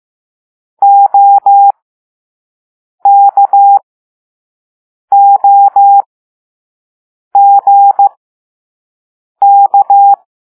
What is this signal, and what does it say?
Morse code, heard as a keyed tone, about 15 wpm, spaced as for 5 wpm, 800 Hz. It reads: OKOGK
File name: Morse / Level4A